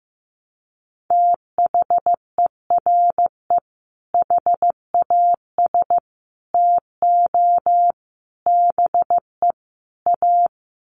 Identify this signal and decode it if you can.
Morse code, heard as a keyed tone, about 15 wpm, 700 Hz. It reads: THERE HAS TO BE A
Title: Sets of 5 Words Encore 15wpm